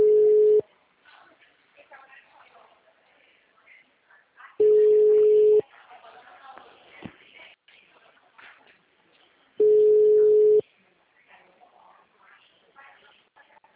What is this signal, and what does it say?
Ringback tone (Europe, 425 Hz), with speech.
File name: out- -2005- - - .73695